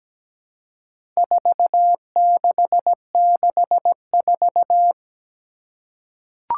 Morse code, heard as a keyed tone, about 17 wpm, 700 Hz. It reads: 4664
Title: Four Challenging Letter-Number Combinations 17wpm